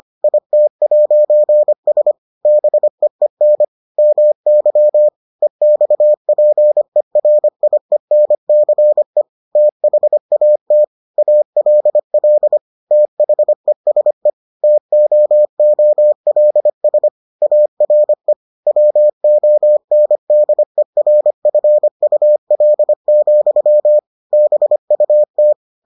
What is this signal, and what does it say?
Morse code, heard as a keyed tone, about 25 wpm, 600 Hz. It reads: IT'S BEEN MY EXPERIENCE THAT ALL THESE TOOLS ARE WONDERFUL, BUT